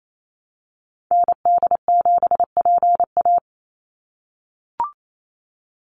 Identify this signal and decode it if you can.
Morse code, heard as a keyed tone, about 28 wpm, 700 Hz. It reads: ND7PA
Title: NA SOTA Activator Callsigns 28wpm